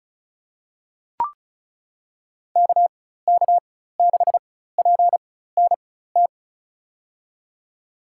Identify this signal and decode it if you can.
Morse code, heard as a keyed tone, about 36 wpm, spaced as for 16 wpm, 700 Hz. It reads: KK6PNT
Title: USA Call Signs - 6 Characters - 4x Character Spacing - 35wpm